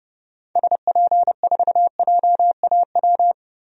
Morse code, heard as a keyed tone, about 30 wpm, 700 Hz. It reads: SP4JAW
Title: DXCC-100 Call Signs - 6 Characters - 30wpm